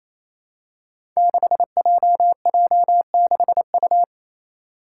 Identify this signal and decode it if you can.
Morse code, heard as a keyed tone, about 28 wpm, 700 Hz. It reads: 6JJ6U
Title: Five Challenging Letter-Number Combinations 28wpm